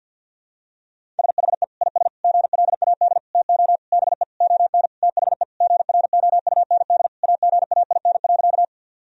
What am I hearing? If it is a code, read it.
Morse code, heard as a keyed tone, about 51 wpm, 700 Hz. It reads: SHE IS GLAD TO BE ON THE GROUND AGAIN.
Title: Sentences from Top 1000 Words - Review Part 2 - 50wpm